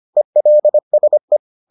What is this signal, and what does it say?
Morse code, heard as a keyed tone, about 25 wpm, 600 Hz. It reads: ELSE